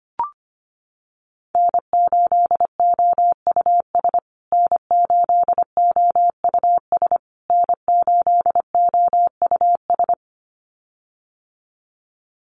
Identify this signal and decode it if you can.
Morse code, heard as a keyed tone, about 25 wpm, 700 Hz. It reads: N8OUS N8OUS N8OUS
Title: USA Call Signs Repeated 3x 25wpm